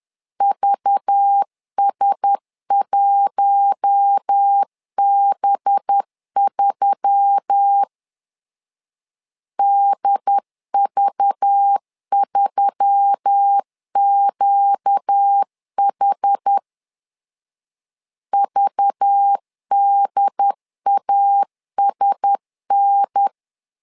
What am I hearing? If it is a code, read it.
Morse code, heard as a keyed tone, about 10 wpm, 800 Hz. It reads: VS1B3 DV3QH VDASN
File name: LEZIONI / 5 LBVC / 5 LBV 10wpm 2248